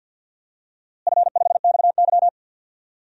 Morse code, heard as a keyed tone, about 52 wpm, 700 Hz. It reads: F5XY
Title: DXCC-100 Call Signs - 4 Characters - Repeated 3x - 50wpm